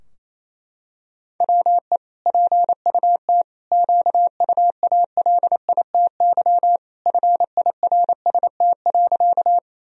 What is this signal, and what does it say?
Morse code, heard as a keyed tone, about 27 wpm, 700 Hz. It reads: WE PUT QUALITY FIRST.
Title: Sentences from Top 1000 Words - Review Part 4 - 28wpm